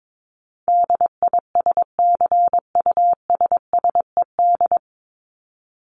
Morse code, heard as a keyed tone, about 22 wpm, 700 Hz. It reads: DISCUSSED
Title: ED Suffix - Top 100 Words with Base Words 22wpm